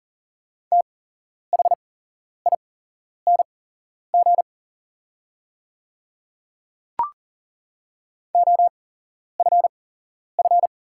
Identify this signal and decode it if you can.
Morse code, heard as a keyed tone, about 40 wpm, spaced as for 11 wpm, 690 Hz. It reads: THING OFF
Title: 8x Character Spacing - Top 500 Words - 40wpm